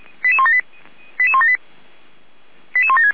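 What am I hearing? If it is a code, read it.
Five-tone selective call (ZVEI), heard as quick runs of five tones, tones of about 70 ms. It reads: 88178 88178 88176
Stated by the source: ZVEI five-tone selective call received via SDR.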